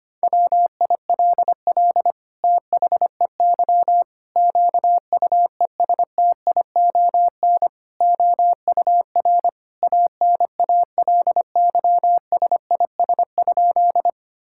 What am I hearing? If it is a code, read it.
Morse code, heard as a keyed tone, about 25 wpm, 700 Hz. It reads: WILL THEY QUESTION OUR ANALYSIS?
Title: Sentences from Top 1000 Words - Review Part 5 - 25wpm